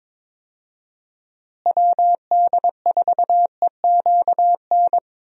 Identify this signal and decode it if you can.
Morse code, heard as a keyed tone, about 21 wpm, 700 Hz. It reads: WD4EQN
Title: USA Call Signs - 6 Characters - Repeated 3x - 22wpm